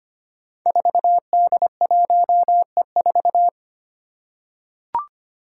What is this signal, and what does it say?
Morse code, heard as a keyed tone, about 25 wpm, 700 Hz. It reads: 4D1E4
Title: Five Challenging Letter-Number Combinations 25wpm